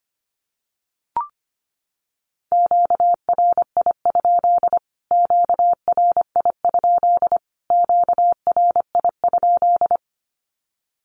Morse code, heard as a keyed tone, about 25 wpm, 700 Hz. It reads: QRI? QRI? QRI?
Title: Q-Codes Repeated 3x 25wpm